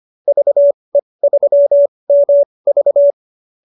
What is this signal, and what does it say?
Morse code, heard as a keyed tone, about 25 wpm, spaced as for 20 wpm, 570 Hz. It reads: VE3MV
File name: qso220b 20